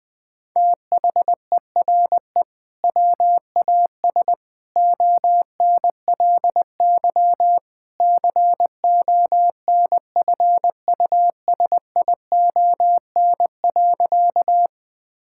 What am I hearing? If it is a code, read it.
Morse code, heard as a keyed tone, about 20 wpm, 700 Hz. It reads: THERE WAS ONLY CONFUSION.
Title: Sentences from Top 1600 Words 20wpm